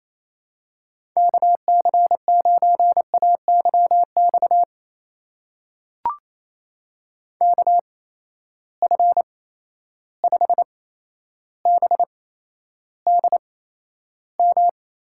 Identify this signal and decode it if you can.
Morse code, heard as a keyed tone, about 28 wpm, spaced as for 12 wpm, 700 Hz. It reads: KC9AYX KF5BDM
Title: USA Call Signs - 6 Characters - 8x Character Spacing - 28wpm